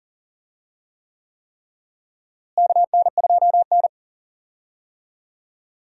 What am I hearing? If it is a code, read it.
Morse code, heard as a keyed tone, about 40 wpm, 700 Hz. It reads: KN2N
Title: USA Call Signs - 4 Characters - Repeated 3x - 40wpm